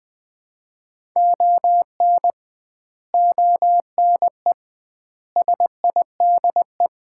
Morse code, heard as a keyed tone, about 20 wpm, 700 Hz. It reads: ON ONE SIDE
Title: 2x Word-Spacing - Sets of 5 Words - 20wpm